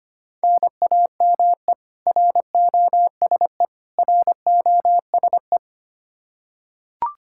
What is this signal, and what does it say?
Morse code, heard as a keyed tone, about 25 wpm, 700 Hz. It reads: NAME ROSE ROSE
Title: Practice QSO - Exchange 1 - Beginning - 25wpm